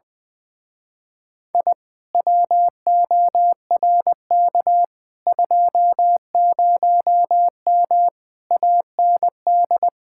Morse code, heard as a keyed tone, about 20 wpm, 700 Hz. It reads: I WORK 20M AND
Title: Speed-Racing Common QSO Phrases - 20wpm